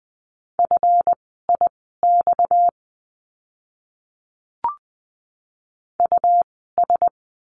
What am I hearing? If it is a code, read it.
Morse code, heard as a keyed tone, about 20 wpm, spaced as for 15 wpm, 700 Hz. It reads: FIX US
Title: 230 Lesson 37f - Letter x - Words 20wpm